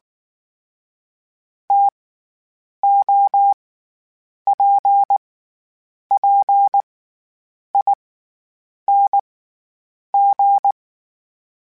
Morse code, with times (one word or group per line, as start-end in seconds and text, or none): 1.7-1.9 T
2.8-3.5 O
4.5-5.2 P
6.1-6.8 P
7.7-7.9 I
8.9-9.2 N
10.1-10.7 G